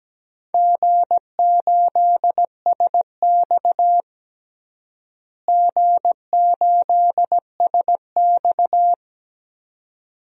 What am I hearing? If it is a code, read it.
Morse code, heard as a keyed tone, about 17 wpm, 700 Hz. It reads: G8SX G8SX
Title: DXCC-100 Call Signs - 4 Characters - Repeated 2x - 17wpm